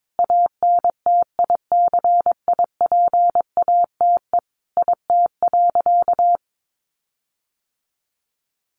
Morse code, heard as a keyed tone, about 22 wpm, 700 Hz. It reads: ANTICIPATE IT.
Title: Sentences from Top 1500 Words 22wpm